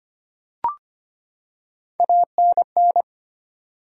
Morse code, heard as a keyed tone, about 25 wpm, 700 Hz. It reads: ANN
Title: USA Common Names - Max 3 Letters - 25wpm